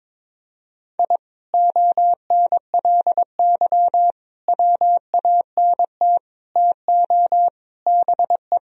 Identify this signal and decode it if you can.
Morse code, heard as a keyed tone, about 22 wpm, 700 Hz. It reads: I ONLY WANT TO BE
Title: Sentences from Top 1000 Words - Review Part 1 - 22wpm